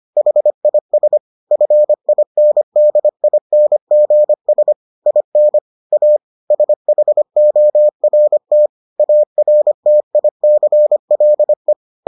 Morse code, heard as a keyed tone, about 25 wpm, 600 Hz. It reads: HIS FINDINGS IN A SHORT ARTICLE